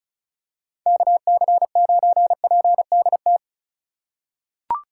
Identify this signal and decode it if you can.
Morse code, heard as a keyed tone, about 36 wpm, 700 Hz. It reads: KC9PDT
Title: USA Call Signs - 6 Characters - 4x Character Spacing - 35wpm